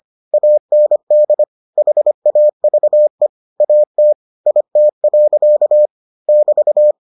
Morse code, heard as a keyed tone, about 25 wpm, 600 Hz. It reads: AND HAVE AT IT. <BT>